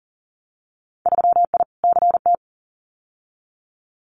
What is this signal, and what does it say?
Morse code, heard as a keyed tone, about 40 wpm, 700 Hz. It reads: 3I CT
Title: Winter Field Day Exchanges 40wpm